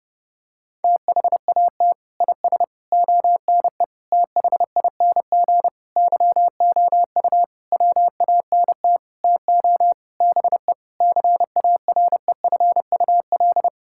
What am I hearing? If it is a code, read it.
Morse code, heard as a keyed tone, about 30 wpm, 700 Hz. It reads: THAT IS ONE THING YOU WANT TO BE CAREFUL
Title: Sentences from Top 1600 Words 30wpm